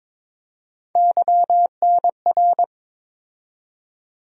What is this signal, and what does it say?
Morse code, heard as a keyed tone, about 22 wpm, 700 Hz. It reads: YNR
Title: USA Call Sign Postfixes - 3 Letters 22wpm